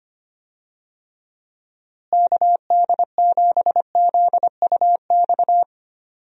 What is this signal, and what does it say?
Morse code, heard as a keyed tone, about 25 wpm, 700 Hz. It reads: KD7ZUX
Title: USA Call Signs - 6 Characters - Repeated 2x - 25wpm